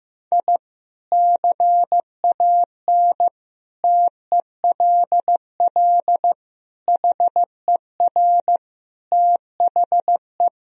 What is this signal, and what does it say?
Morse code, heard as a keyed tone, about 15 wpm, 700 Hz. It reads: I CAN TELL HER THE
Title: Sentences from Top 1000 Words - Review Part 3 - 15wpm